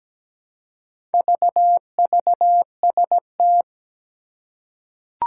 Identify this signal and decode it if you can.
Morse code, heard as a keyed tone, about 17 wpm, 700 Hz. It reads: VVST
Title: Four Challenging Letter-Number Combinations 17wpm